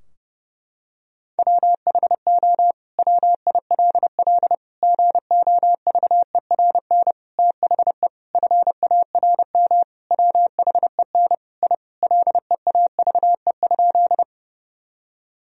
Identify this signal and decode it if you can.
Morse code, heard as a keyed tone, about 32 wpm, 700 Hz. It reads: WHO WILL GOVERN THE FARM WHEN I LEAVE?
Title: Sentences from Top 1000 Words - Review Part 2 - 30wpm